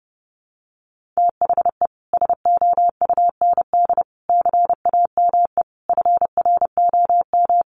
Morse code, heard as a keyed tone, about 30 wpm, 700 Hz. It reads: THE SOUND CAME FROM
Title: Sentences from Top 1400 Words 30wpm